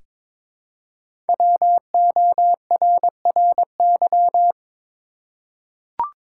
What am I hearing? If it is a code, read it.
Morse code, heard as a keyed tone, about 22 wpm, 700 Hz. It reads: WORRY
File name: 5 Letter Words - First 3 Letters Given - 22wpm